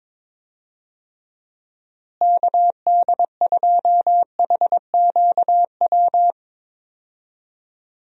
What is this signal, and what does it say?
Morse code, heard as a keyed tone, about 22 wpm, 700 Hz. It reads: KD2HQW
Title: USA Call Signs - 6 Characters - Repeated 2x - 22wpm